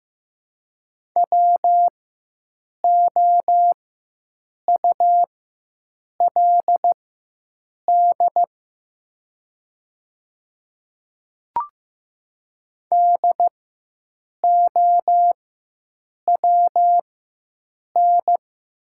Morse code, heard as a keyed tone, about 15 wpm, spaced as for 7 wpm, 700 Hz. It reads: WOULD DOWN
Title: 4x Character Spacing - Top 300 Words - 15wpm